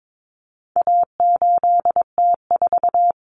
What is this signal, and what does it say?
Morse code, heard as a keyed tone, about 22 wpm, 700 Hz. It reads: A8T4
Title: Common Cut Numbers - 4-Digit 22wpm